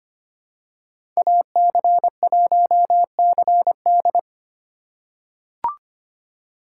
Morse code, heard as a keyed tone, about 25 wpm, 700 Hz. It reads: AC1CD
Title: USA Call Signs - 2x Character Spacing - 25wpm